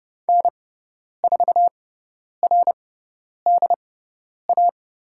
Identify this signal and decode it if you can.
Morse code, heard as a keyed tone, about 30 wpm, spaced as for 10 wpm, 700 Hz. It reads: N4RDA
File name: Lesson 18g - Letter m - Call Signs 30wpm